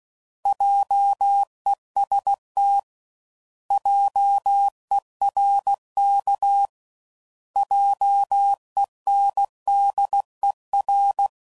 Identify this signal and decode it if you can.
Morse code, heard as a keyed tone, about 16 wpm, 800 Hz. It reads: JEST JERK JENDER